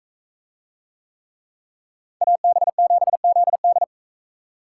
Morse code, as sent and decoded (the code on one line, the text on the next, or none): .- -... --... --.. -..
AB7ZD